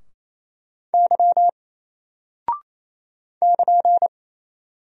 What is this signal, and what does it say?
Morse code, heard as a keyed tone, about 27 wpm, 700 Hz. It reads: Y <KN>
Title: Single Character (Letter:Number:Punctuation:Prosign) 28wpm